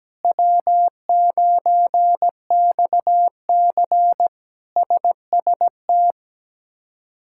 Morse code, heard as a keyed tone, about 17 wpm, 700 Hz. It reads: W9XC SST
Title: K1USN SST QSO Practice - 17wpm